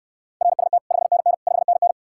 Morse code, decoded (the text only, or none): RST 5NN 5NN